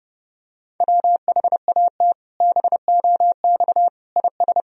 Morse code, heard as a keyed tone, about 30 wpm, 700 Hz. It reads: WHAT BOX IS
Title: Sentences from Top 1000 Words - Review Part 2 - 30wpm